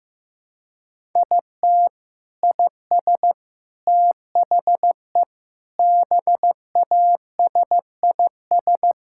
Morse code, heard as a keyed tone, about 15 wpm, 700 Hz. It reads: IT IS THE BASIS
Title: Sentences from Top 1000 Words - Review Part 5 - 15wpm